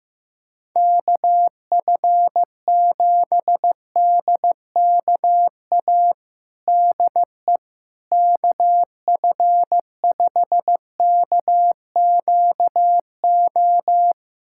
Morse code, heard as a keyed tone, about 15 wpm, 700 Hz. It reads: KF7DKA DE KF5KQO
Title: USA Callsign DE Callsign - 6 Characters - 15wpm